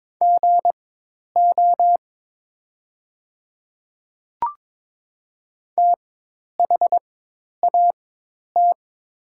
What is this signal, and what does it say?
Morse code, heard as a keyed tone, about 22 wpm, spaced as for 10 wpm, 700 Hz. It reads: GO THAT
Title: 4x Character Spacing - Top 200 Words - 22wpm